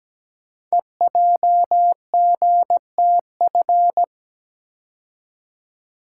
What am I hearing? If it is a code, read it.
Morse code, heard as a keyed tone, about 17 wpm, 700 Hz. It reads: EJGTF
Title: Code Groups - 5 Letters - 17wpm